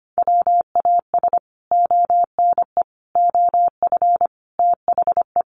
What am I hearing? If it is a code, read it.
Morse code, heard as a keyed tone, about 25 wpm, 700 Hz. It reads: WAS ONE OF THE
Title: Sets of 5 Words Encore2 25wpm